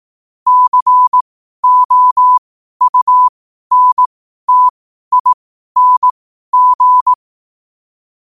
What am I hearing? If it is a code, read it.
Morse code, heard as a keyed tone, about 18 wpm, spaced as for 12 wpm, 1000 Hz. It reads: COUNTING
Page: 12 wpm morse code quotes